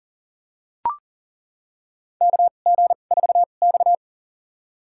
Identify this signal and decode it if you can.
Morse code, heard as a keyed tone, about 40 wpm, spaced as for 29 wpm, 700 Hz. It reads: KG4X
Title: USA Call Signs - 2x Character Spacing - 40wpm